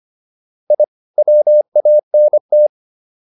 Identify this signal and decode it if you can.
Morse code, heard as a keyed tone, about 25 wpm, 600 Hz. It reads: I WANT
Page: intermediate-practice-files